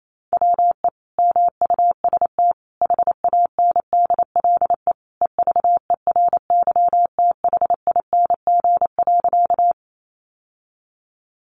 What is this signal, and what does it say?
Morse code, heard as a keyed tone, about 28 wpm, 700 Hz. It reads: WE MUST HANDLE EVERYTHING.
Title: Sentences from Top 1100 Words 28wpm